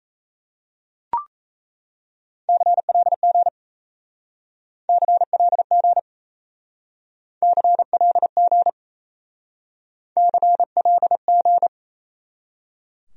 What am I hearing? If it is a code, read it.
Morse code, heard as a keyed tone, about 36 wpm, 700 Hz. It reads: CLG CLG CLG CLG